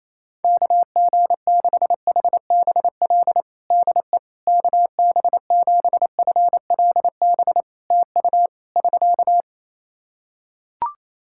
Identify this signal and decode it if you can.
Morse code, heard as a keyed tone, about 28 wpm, 700 Hz. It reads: KG6HBL DE KB7FLB TU <SK>
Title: Practice QSO - Exchange 4 - Ending - 28wpm